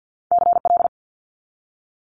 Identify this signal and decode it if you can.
Morse code, heard as a keyed tone, about 50 wpm, 700 Hz. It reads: CL
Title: Multi-letter Phonemes 50wpm